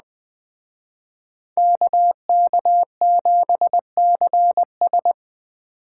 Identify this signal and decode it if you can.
Morse code, heard as a keyed tone, about 20 wpm, 700 Hz. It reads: KK7CS